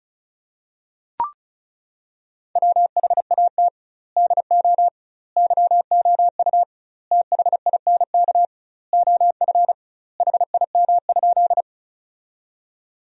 Morse code, heard as a keyed tone, about 36 wpm, 700 Hz. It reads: WHAT DO YOU THINK OF HIM?
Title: 2x Word-Spacing - Sentences from Top 200 Words - 35wpm